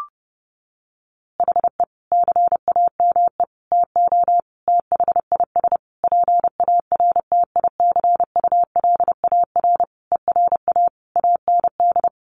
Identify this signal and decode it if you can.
Morse code, heard as a keyed tone, about 30 wpm, 700 Hz. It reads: HE CAME TO THIS PARTICULAR ERA AND
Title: A Sound Of Thunder 30wpm